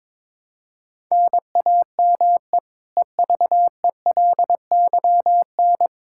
Morse code, heard as a keyed tone, about 22 wpm, 700 Hz. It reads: NAME EVELYN
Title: Practice QSO - Exchange 1 - Beginning - 22wpm